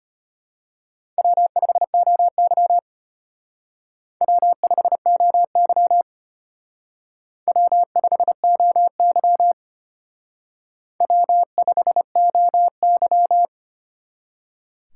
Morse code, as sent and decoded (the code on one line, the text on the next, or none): .-- ..... --- -.-- / .-- ..... --- -.-- / .-- ..... --- -.-- / .-- ..... --- -.--
W5OY W5OY W5OY W5OY